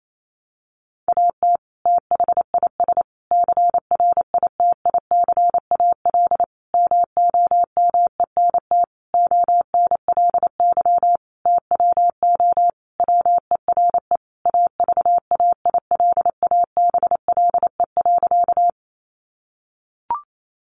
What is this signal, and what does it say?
Morse code, heard as a keyed tone, about 28 wpm, 700 Hz. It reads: AT THIS CRITICAL MOMENT ONLY TWO WERE AVAILABLE.
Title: Sentences from Top 1100 Words 28wpm